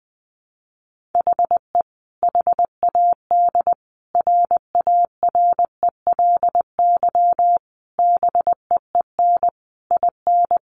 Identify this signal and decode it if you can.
Morse code, heard as a keyed tone, about 20 wpm, 700 Hz. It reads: HE HAD RARELY BEEN IN
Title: Sentences from Top 1400 Words 20wpm